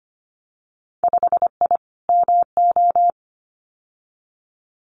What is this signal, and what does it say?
Morse code, heard as a keyed tone, about 25 wpm, 700 Hz. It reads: 5I MO
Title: Winter Field Day Exchanges Encore 25wpm